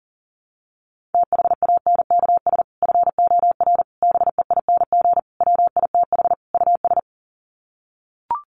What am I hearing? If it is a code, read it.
Morse code, heard as a keyed tone, about 40 wpm, 700 Hz. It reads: THANKS FOR BEING WITH US
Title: Sets of 5 Words 40wpm